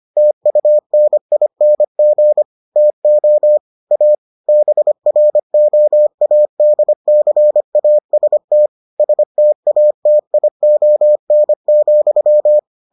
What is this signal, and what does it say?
Morse code, heard as a keyed tone, about 25 wpm, 600 Hz. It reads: TUNING TO A BROADCAST STATION,